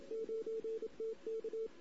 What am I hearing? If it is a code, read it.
Morse code, heard as a keyed tone, about 27 wpm, 450 Hz. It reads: 9TK
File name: CW / 2008-ss-cw-k5tr-op-n5rz -16:00